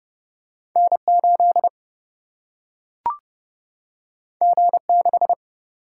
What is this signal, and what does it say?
Morse code, heard as a keyed tone, about 30 wpm, 700 Hz. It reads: N 8 G 6
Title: Single-Letter Single-Number 30wpm